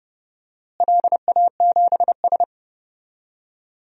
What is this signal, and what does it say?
Morse code, heard as a keyed tone, about 30 wpm, 700 Hz. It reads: LA7S
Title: DXCC-100 Call Signs - 4 Characters - Repeated 3x - 30wpm